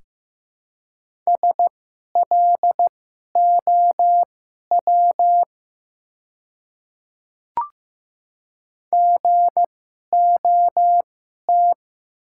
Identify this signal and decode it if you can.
Morse code, heard as a keyed tone, about 15 wpm, spaced as for 11 wpm, 700 Hz. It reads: SLOW GOT
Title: 2x Character Spacing - Top 500 Words - 15wpm